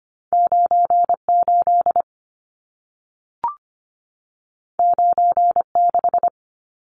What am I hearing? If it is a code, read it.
Morse code, heard as a keyed tone, about 25 wpm, 700 Hz. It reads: 98 96
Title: Numbers - 2-Digit 25wpm